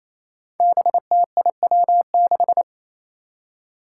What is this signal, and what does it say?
Morse code, heard as a keyed tone, about 28 wpm, 700 Hz. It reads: BTIW6
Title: Five Challenging Letter-Number Combinations 28wpm